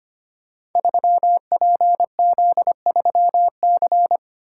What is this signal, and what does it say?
Morse code, heard as a keyed tone, about 25 wpm, 700 Hz. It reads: 3PZ3C
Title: Code Groups - 5 Letter:Numbers - 25wpm